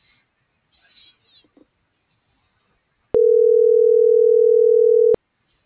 Ringback tone (North America), 440 + 480 Hz.